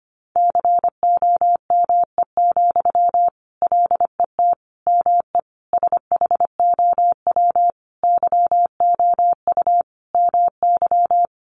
Morse code, heard as a keyed tone, about 25 wpm, 700 Hz. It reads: COME, LET ME SHOW YOU MY
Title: Sentences from Top 500 Words 25wpm